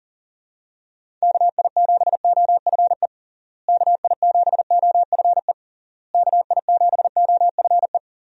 Morse code, heard as a keyed tone, about 37 wpm, 700 Hz. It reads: KI7OFE KI7OFE KI7OFE
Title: USA Call Signs - 6 Characters - Repeated 3x - 40wpm